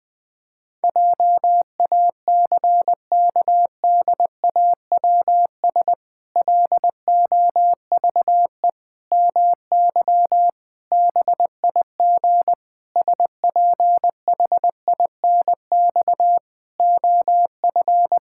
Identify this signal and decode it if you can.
Morse code, heard as a keyed tone, about 21 wpm, 700 Hz. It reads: JACKDAWS LOVE MY BIG SPHINX OF
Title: Pangram Challenge v2 - 20wpm